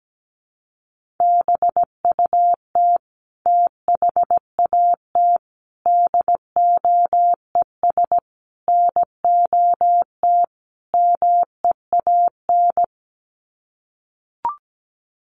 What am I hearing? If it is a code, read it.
Morse code, heard as a keyed tone, about 17 wpm, 700 Hz. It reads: BUT THAT DOES NOT MEAN
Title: Sets of 5 Words 17wpm